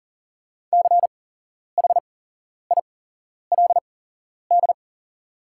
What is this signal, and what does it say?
Morse code, heard as a keyed tone, about 40 wpm, spaced as for 11 wpm, 700 Hz. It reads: CHILD
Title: 8x Character Spacing - Top 300 Words - 40wpm